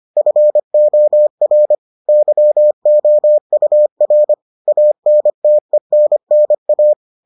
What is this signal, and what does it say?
Morse code, heard as a keyed tone, about 25 wpm, 600 Hz. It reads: FOR YOUR ANTENNA